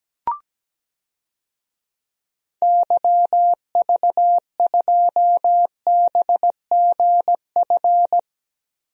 Morse code, heard as a keyed tone, about 17 wpm, 700 Hz. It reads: YV2BGF
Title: DXCC-100 Call Signs - 6 Characters - Repeated 2x - 17wpm